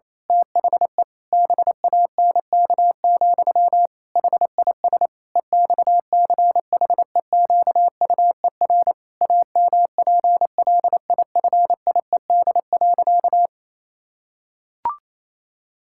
Morse code, heard as a keyed tone, about 28 wpm, 700 Hz. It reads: THE BANK, HIS EXCHEQUER AMPLIFIED.
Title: Pangram Challenge v2 - 28wpm